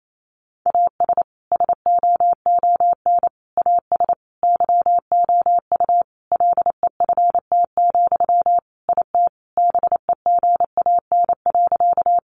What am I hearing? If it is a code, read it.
Morse code, heard as a keyed tone, about 28 wpm, 700 Hz. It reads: AS SOON AS YOU LEFT, IT BEGAN.
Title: Sentences from Top 400 Words 28wpm